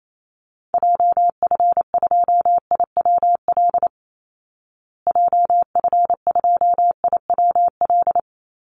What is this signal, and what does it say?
Morse code, heard as a keyed tone, about 28 wpm, 700 Hz. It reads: JF2IWL JF2IWL
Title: Call Signs Repeated 3x - 28wpm